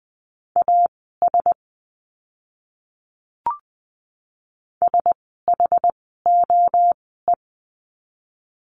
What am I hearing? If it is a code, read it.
Morse code, heard as a keyed tone, about 20 wpm, spaced as for 15 wpm, 700 Hz. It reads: AS SHOE
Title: 061 Lesson 11f - Letter h - Words 20wpm